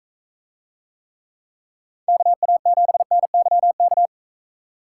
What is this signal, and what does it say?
Morse code, heard as a keyed tone, about 40 wpm, 700 Hz. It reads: KA7NYK